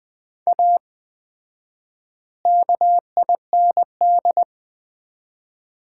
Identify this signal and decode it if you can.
Morse code, heard as a keyed tone, about 20 wpm, 700 Hz. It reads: A KIND
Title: 4x Word-Spacing - Sets of 3 Words - 20wpm